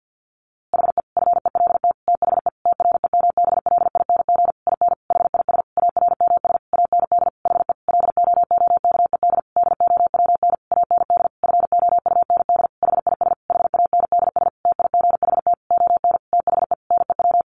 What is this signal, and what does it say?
Morse code, heard as a keyed tone, about 50 wpm, 700 Hz. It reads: HE FELT THE TREMBLING IN HIS ARMS AND HE LOOKED DOWN AND FOUND HIS HANDS TIGHT ON THE NEW
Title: A Sound Of Thunder 50wpm